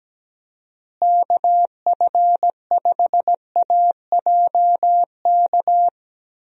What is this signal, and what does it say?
Morse code, heard as a keyed tone, about 17 wpm, 700 Hz. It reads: KF5AJK
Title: USA Call Signs - 6 Characters - Repeated 2x - 17wpm